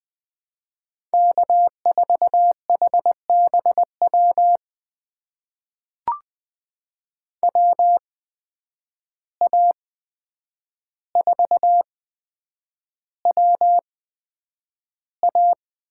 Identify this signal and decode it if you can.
Morse code, heard as a keyed tone, about 20 wpm, spaced as for 9 wpm, 700 Hz. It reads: K4HBW WA4WA
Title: USA Call Signs - 5 Characters - 8x Character Spacing - 20wpm